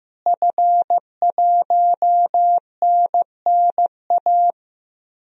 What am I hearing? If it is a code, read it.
Morse code, heard as a keyed tone, about 15 wpm, 700 Hz. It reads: F1NNA
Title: DXCC-100 Call Signs - 5 Characters - Repeated 3x - 15wpm